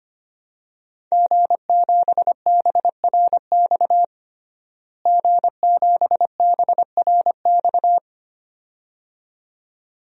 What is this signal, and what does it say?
Morse code, heard as a keyed tone, about 25 wpm, 700 Hz. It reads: G7BRX G7BRX
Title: DXCC-100 Call Signs - 5 Characters - Repeated 2x - 25wpm